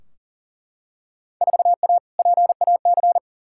Morse code, heard as a keyed tone, about 40 wpm, 700 Hz. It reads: 4A PAC
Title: Field Day Exchanges Encore - 40wpm